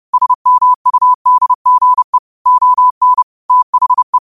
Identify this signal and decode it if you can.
Morse code, heard as a keyed tone, about 30 wpm, 1000 Hz. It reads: SMUDGE ON THE
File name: qotdfiles / 30wpm